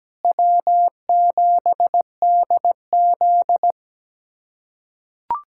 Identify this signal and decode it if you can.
Morse code, heard as a keyed tone, about 17 wpm, 700 Hz. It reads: W7DZ
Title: USA Callsign DE Callsign - 4 Characters - 17wpm